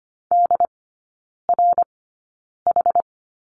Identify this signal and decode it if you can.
Morse code, heard as a keyed tone, about 25 wpm, spaced as for 9 wpm, 690 Hz. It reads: DRH
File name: Lesson 14e - Number 2 - Three Characters 25wpm